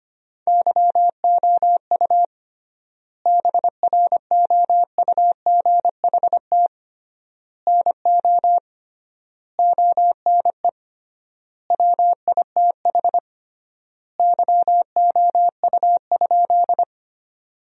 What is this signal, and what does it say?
Morse code, heard as a keyed tone, about 25 wpm, 700 Hz. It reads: YOU BROUGHT NO ONE WITH YOU?
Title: 3x Word-Spacing - Sentences from Top 500 Words - 25wpm